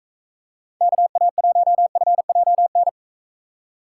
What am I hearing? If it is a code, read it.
Morse code, heard as a keyed tone, about 43 wpm, 700 Hz. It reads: KA1FJN